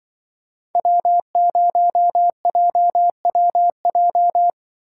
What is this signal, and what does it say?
Morse code, heard as a keyed tone, about 25 wpm, 700 Hz. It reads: W0JWJ
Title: Speed Racing - USA 1x3 Call Signs 20wpm